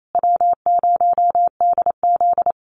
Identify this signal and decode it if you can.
Morse code, heard as a keyed tone, about 28 wpm, 700 Hz. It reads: W0DZ
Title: USA Call Signs Repeated 2x 28wpm